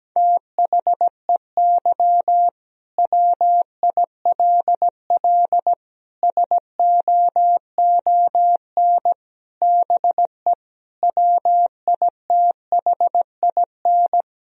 Morse code, heard as a keyed tone, about 17 wpm, 700 Hz. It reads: THEY WILL SOON BE WITHIN
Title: Sentences from Top 1000 Words - Review Part 4 - 17wpm